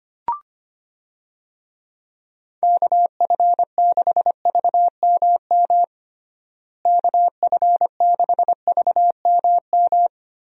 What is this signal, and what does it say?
Morse code, heard as a keyed tone, about 25 wpm, 700 Hz. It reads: KF6VMM KF6VMM
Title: USA Call Signs - 6 Characters - Repeated 3x - 25wpm